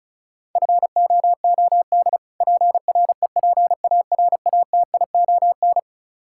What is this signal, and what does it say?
Morse code, heard as a keyed tone, about 34 wpm, 700 Hz. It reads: FOOD PREPARATION
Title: Practice QSO - Exchange 3 - More Info - 35wpm